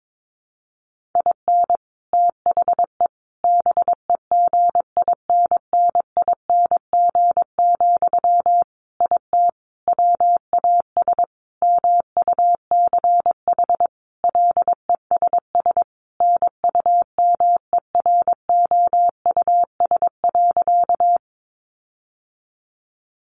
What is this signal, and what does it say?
Morse code, heard as a keyed tone, about 22 wpm, 700 Hz. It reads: IN THE BEGINNING, IT WAS MUCH LESS NUMEROUS.
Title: Sentences from Top 1400 Words 22wpm